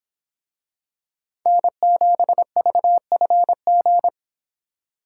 Morse code, heard as a keyed tone, about 25 wpm, 700 Hz. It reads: N7VFG